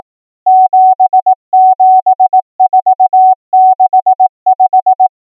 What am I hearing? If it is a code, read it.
Morse code, heard as a keyed tone, about 18 wpm, 750 Hz. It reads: 77465
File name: numbers / V2-morse